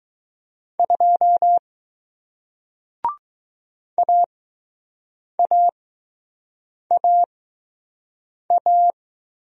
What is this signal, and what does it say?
Morse code, heard as a keyed tone, about 21 wpm, spaced as for 6 wpm, 700 Hz. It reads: 2 AAAA